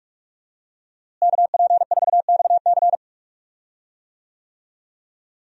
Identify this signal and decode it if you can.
Morse code, heard as a keyed tone, about 46 wpm, 700 Hz. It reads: KP4XC
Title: DXCC-100 Call Signs - 5 Characters - Repeated 3x - 45wpm